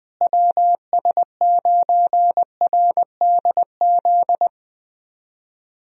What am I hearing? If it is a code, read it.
Morse code, heard as a keyed tone, about 20 wpm, 700 Hz. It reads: WS9RDZ
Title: USA Call Signs - 6 Characters - Repeated 2x - 20wpm